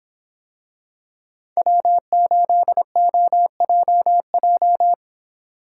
Morse code, heard as a keyed tone, about 26 wpm, 700 Hz. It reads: W8OJJ